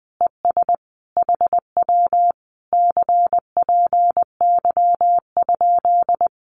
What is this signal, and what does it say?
Morse code, heard as a keyed tone, about 20 wpm, 700 Hz. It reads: ES HW CPY?
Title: Common QSO Phrases Encore 20wpm